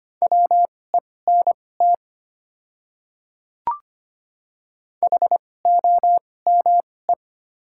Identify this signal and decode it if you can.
Morse code, heard as a keyed tone, about 25 wpm, spaced as for 18 wpm, 690 Hz. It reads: WENT HOME
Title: 2x Character Spacing - Top 300 Words - 25wpm